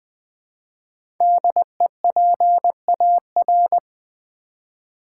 Morse code, heard as a keyed tone, about 20 wpm, 700 Hz. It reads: DEPAR
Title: Build A Word - 5 Letters At A Time - 10-Letter Words - 20wpm